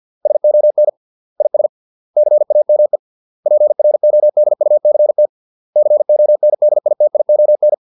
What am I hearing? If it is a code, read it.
Morse code, heard as a keyed tone, about 50 wpm, 600 Hz. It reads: SON IS CAME PRODUCT CONDITION